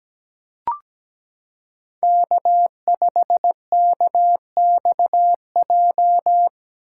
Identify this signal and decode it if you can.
Morse code, heard as a keyed tone, about 17 wpm, 700 Hz. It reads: K5KXJ
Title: K1USN SST QSO Practice - 17wpm